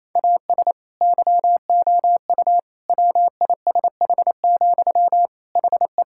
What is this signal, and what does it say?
Morse code, heard as a keyed tone, about 28 wpm, 700 Hz. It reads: AS YOU WISH, HE
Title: Sentences from Top 1600 Words 28wpm